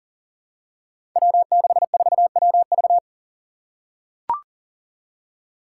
Morse code, heard as a keyed tone, about 40 wpm, 700 Hz. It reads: W64WV
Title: Five Challenging Letter-Number Combinations 40wpm